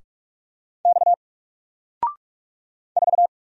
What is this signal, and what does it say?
Morse code, heard as a keyed tone, about 44 wpm, 700 Hz. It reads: X 4
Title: Single Letter-Number Alt 45wpm